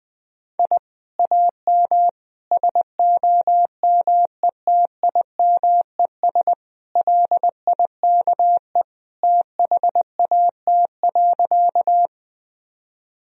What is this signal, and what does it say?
Morse code, heard as a keyed tone, about 21 wpm, 700 Hz. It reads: I AM SOMETIMES LIKE THAT.
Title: Sentences from Top 1000 Words - Review Part 3 - 20wpm